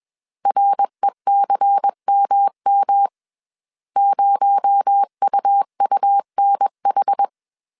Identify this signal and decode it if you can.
Morse code, heard as a keyed tone, about 21 wpm, 800 Hz. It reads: RECMM 0UUNH
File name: CW / DET / 1 6 1500C S C 20wpm 2116